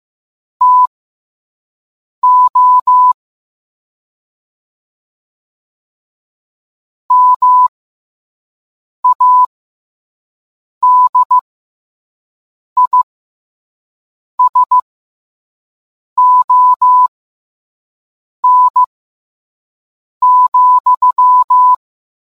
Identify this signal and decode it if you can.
Morse code, heard as a keyed tone, about 15 wpm, spaced as for 5 wpm, 1000 Hz. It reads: TO MADISON,